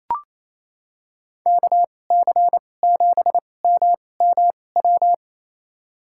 Morse code, heard as a keyed tone, about 28 wpm, spaced as for 20 wpm, 700 Hz. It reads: KC7MMW
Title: USA Call Signs - 6 Characters - 2x Character Spacing - 28wpm